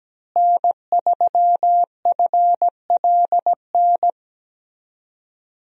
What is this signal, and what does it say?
Morse code, heard as a keyed tone, about 17 wpm, 700 Hz. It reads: N3FLN
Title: USA Call Signs - 5 Characters - Repeated 2x - 17wpm